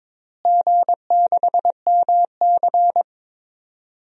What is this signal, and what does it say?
Morse code, heard as a keyed tone, about 22 wpm, 700 Hz. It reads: G6MC
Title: DXCC-100 Call Signs - 4 Characters - Repeated 2x - 22wpm